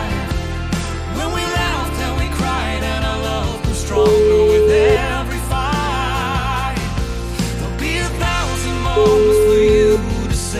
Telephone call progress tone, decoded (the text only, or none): ringback tone